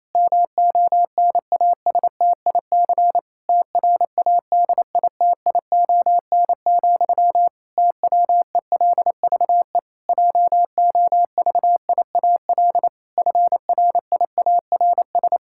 Morse code, heard as a keyed tone, about 28 wpm, 700 Hz. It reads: MONASTIC TRADITION, TWELVE JOVIAL FRIARS
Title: Pangram Challenge v2 - 28wpm